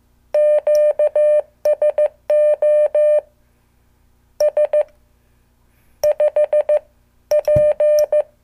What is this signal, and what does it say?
Morse code, heard as a keyed tone, about 15 wpm, 600 Hz. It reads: QSO S 5 P